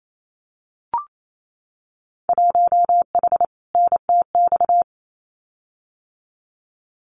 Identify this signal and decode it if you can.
Morse code, heard as a keyed tone, about 28 wpm, 700 Hz. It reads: 1H NTX
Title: Winter Field Day Exchanges 28wpm